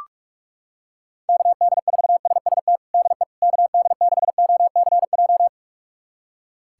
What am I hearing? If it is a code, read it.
Morse code, heard as a keyed tone, about 46 wpm, 700 Hz. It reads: KD4SST DE KD6OCJ
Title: USA Callsign DE Callsign - 6 Characters - 45wpm